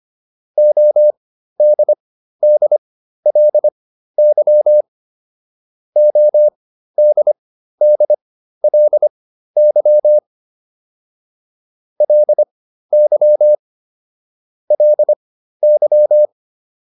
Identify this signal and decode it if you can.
Morse code, heard as a keyed tone, about 25 wpm, spaced as for 13 wpm, 600 Hz. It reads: ODDLY ODDLY LY LY
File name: LY5 13